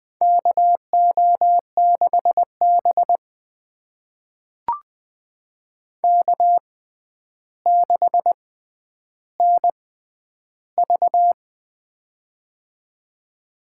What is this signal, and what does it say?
Morse code, heard as a keyed tone, about 21 wpm, spaced as for 10 wpm, 700 Hz. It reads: KO6B K6NV
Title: USA Call Signs - 4 Characters - 6x Character Spacing - 20wpm